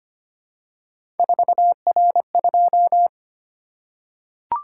Morse code, heard as a keyed tone, about 25 wpm, 700 Hz. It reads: 4R2
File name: Three Letter-Number Combinations - 25wpm